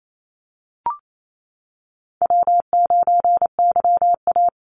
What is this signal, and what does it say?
Morse code, heard as a keyed tone, about 28 wpm, 700 Hz. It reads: W9YA
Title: Easy Call Signs - 28wpm